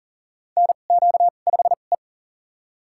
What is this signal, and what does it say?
Morse code, heard as a keyed tone, about 39 wpm, spaced as for 29 wpm, 700 Hz. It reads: NQ5E
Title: USA Call Signs - 2x Character Spacing - 40wpm